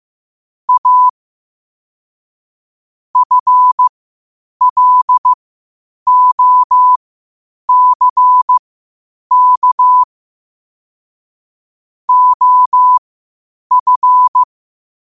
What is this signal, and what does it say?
Morse code, heard as a keyed tone, about 15 wpm, spaced as for 8 wpm, 1000 Hz. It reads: A FLOCK OF
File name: qotdfiles / 8wpm